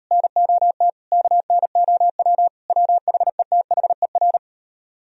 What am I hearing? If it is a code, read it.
Morse code, heard as a keyed tone, about 37 wpm, 700 Hz. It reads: NOT KNOW WHETHER